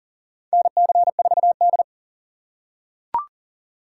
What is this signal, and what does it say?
Morse code, heard as a keyed tone, about 37 wpm, 700 Hz. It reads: NC4D
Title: USA Call Signs - 4 Characters - 4x Character Spacing - 40wpm